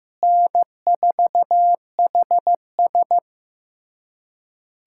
Morse code, heard as a keyed tone, about 15 wpm, 700 Hz. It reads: N4HS
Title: USA Callsign DE Callsign - 4 Characters - 15wpm